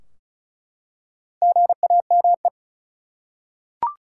Morse code, heard as a keyed tone, about 34 wpm, 700 Hz. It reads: GAME
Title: Four Letter Words - 35wpm